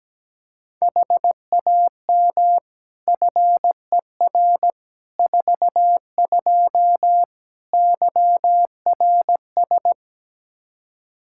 Morse code, heard as a keyed tone, about 17 wpm, 700 Hz. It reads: HAM FER 42 YRS
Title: Practice QSO - Exchange 3 - More Info - 17wpm